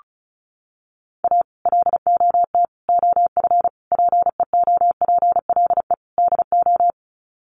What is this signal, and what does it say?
Morse code, heard as a keyed tone, about 34 wpm, 700 Hz. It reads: A LOT OF PEOPLE DO
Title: Sets of 5 Words 35wpm